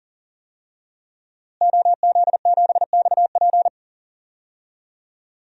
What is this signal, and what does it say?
Morse code, heard as a keyed tone, about 37 wpm, 700 Hz. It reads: OZ7XP
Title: DXCC-100 Call Signs - 5 Characters - Repeated 2x - 40wpm